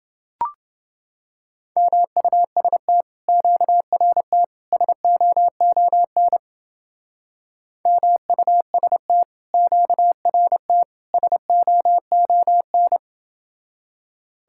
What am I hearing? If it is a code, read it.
Morse code, heard as a keyed tone, about 30 wpm, 700 Hz. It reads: MUST QRT SOON MUST QRT SOON